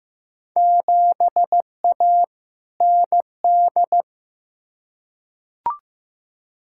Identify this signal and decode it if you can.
Morse code, heard as a keyed tone, about 15 wpm, 700 Hz. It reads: 7A ND
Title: Field Day Exchanges Encore - 15wpm